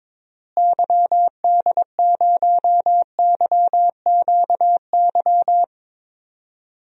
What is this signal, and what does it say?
Morse code, heard as a keyed tone, about 22 wpm, 700 Hz. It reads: YD0YQY
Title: DXCC-100 Call Signs - 6 Characters - Repeated 2x - 22wpm